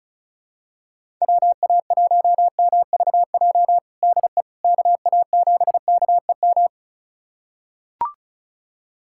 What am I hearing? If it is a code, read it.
Morse code, heard as a keyed tone, about 34 wpm, 700 Hz. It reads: WA1MVJ DE KA7KEM
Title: USA Callsign DE Callsign - 6 Characters - 35wpm